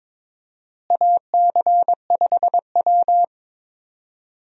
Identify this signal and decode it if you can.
Morse code, heard as a keyed tone, about 22 wpm, 700 Hz. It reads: AC5W
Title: USA Call Signs - 4 Characters - Repeated 2x - 22wpm